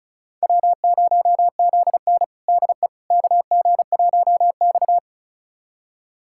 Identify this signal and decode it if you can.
Morse code, heard as a keyed tone, about 34 wpm, 700 Hz. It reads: W0ZN DE KG1X
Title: USA Callsign DE Callsign - 4 Characters - 35wpm